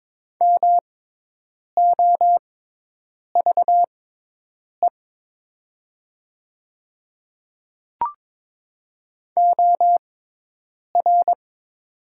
Morse code, heard as a keyed tone, about 22 wpm, spaced as for 8 wpm, 700 Hz. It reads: MOVE OR
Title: 6x Character Spacing - Top 200 Words - 22wpm